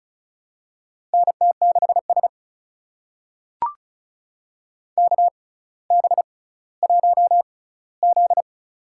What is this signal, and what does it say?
Morse code, heard as a keyed tone, about 36 wpm, spaced as for 18 wpm, 700 Hz. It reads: NT6S KB1Z
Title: USA Call Signs - 4 Characters - 6x Character Spacing - 35wpm